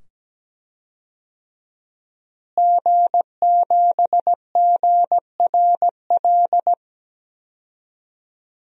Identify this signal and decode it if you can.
Morse code, heard as a keyed tone, about 17 wpm, 700 Hz. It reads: G7GRL
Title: DXCC-100 Call Signs - 5 Characters - Repeated 2x - 17wpm